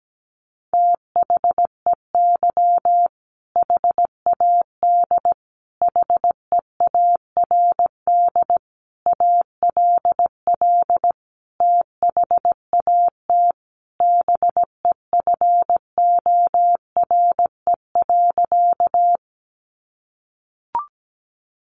Morse code, heard as a keyed tone, about 17 wpm, 700 Hz. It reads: THEY HAD HEARD ALL THAT BEFORE.
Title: Sentences from Top 400 Words 17wpm